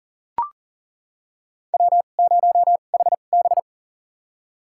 Morse code, heard as a keyed tone, about 39 wpm, spaced as for 28 wpm, 690 Hz. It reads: W0HB
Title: USA Call Signs - 4 Characters - 2x Character Spacing - 40wpm